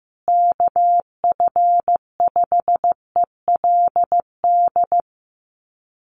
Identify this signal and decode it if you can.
Morse code, heard as a keyed tone, about 15 wpm, 700 Hz. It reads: KF5ELD
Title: USA Call Signs Repeated 2x 15wpm